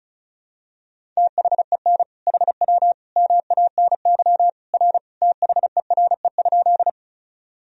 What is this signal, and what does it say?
Morse code, heard as a keyed tone, about 34 wpm, 700 Hz. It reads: THEN HW MANY R THERE?
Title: Sentences from Top 100 Words and Abbreviations 35wpm